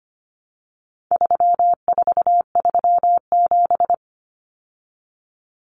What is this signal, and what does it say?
Morse code, heard as a keyed tone, about 25 wpm, 700 Hz. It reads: 3437
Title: Common Cut Numbers - 4-Digit 25wpm